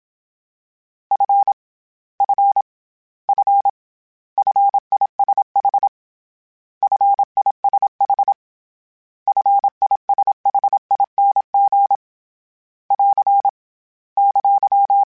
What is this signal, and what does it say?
Morse code, heard as a keyed tone, about 27 wpm, 800 Hz. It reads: F F F FISH FISH FISHING <AR> !
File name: Short Runs / Short Runs-mp3s / SinglePrefixes @25